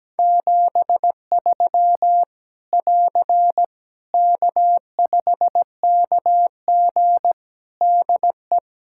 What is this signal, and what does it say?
Morse code, heard as a keyed tone, about 17 wpm, 700 Hz. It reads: 73 <AR> K5KG DE
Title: K1USN SST QSO Practice 17wpm